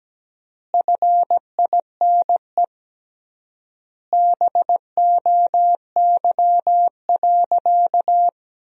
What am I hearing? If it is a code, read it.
Morse code, heard as a keyed tone, about 17 wpm, 700 Hz. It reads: FINE BOY.
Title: 3x Word-Spacing - Sentences from Top 500 Words - 17wpm